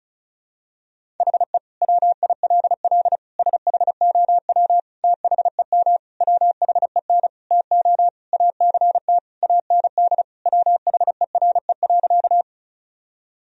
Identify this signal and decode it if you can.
Morse code, heard as a keyed tone, about 34 wpm, 700 Hz. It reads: HE WILL SHOW THEM WHEN TO ACT AND WHERE.
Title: Sentences from Top 1000 Words - Review Part 1 - 35wpm